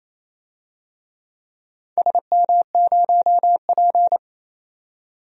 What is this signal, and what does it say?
Morse code, heard as a keyed tone, about 28 wpm, 700 Hz. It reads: SM0P
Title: DXCC-100 Call Signs - 4 Characters - 28wpm